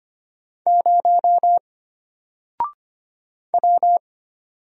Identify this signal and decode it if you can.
Morse code, heard as a keyed tone, about 25 wpm, 700 Hz. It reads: 0 W
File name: Single-Letter-Number-25wpm